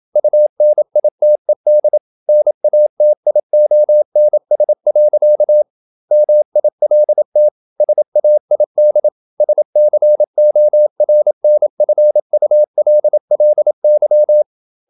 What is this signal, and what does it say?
Morse code, heard as a keyed tone, about 27 wpm, 600 Hz. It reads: UNITED NATIONS. MILT SAID SCORNFULLY
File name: Never 0027wpm